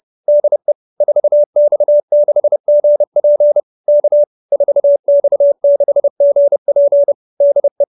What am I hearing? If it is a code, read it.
Morse code, heard as a keyed tone, about 30 wpm, 570 Hz. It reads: DE 4X6GP K 4X6GP DE
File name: QSO310 30